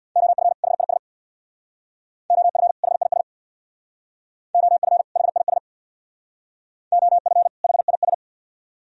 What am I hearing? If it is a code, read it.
Morse code, heard as a keyed tone, about 61 wpm, 700 Hz. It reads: OF HIS OF HIS OF HIS OF HIS
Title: Speed-Racing - Sets of 2 Words 50wpm